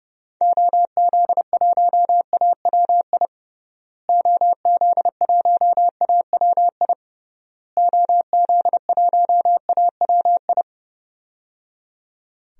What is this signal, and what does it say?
Morse code, heard as a keyed tone, about 31 wpm, 700 Hz. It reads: OZ1AWI OZ1AWI OZ1AWI
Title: DXCC-100 Call Signs - 6 Characters - Repeated 3x - 30wpm